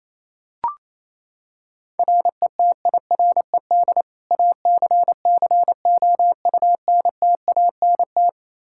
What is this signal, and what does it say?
Morse code, heard as a keyed tone, about 27 wpm, 700 Hz. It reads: RETIRED ACCOUNTANT
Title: Practice QSO - Exchange 3 - More Info - 28wpm